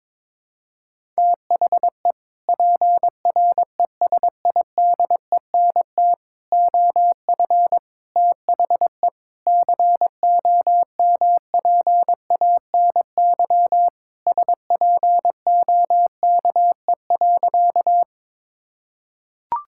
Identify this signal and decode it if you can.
Morse code, heard as a keyed tone, about 22 wpm, 700 Hz. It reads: THE PRESIDENT OF THE COMPANY SPOKE.
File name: practice / 22wpm / Sentences with 3-Word Phrases - 22wpm